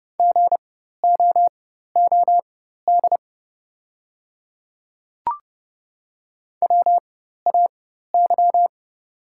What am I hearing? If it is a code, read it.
Morse code, heard as a keyed tone, about 30 wpm, spaced as for 14 wpm, 700 Hz. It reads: GOOD WAY
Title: 4x Character Spacing - Top 200 Words - 30wpm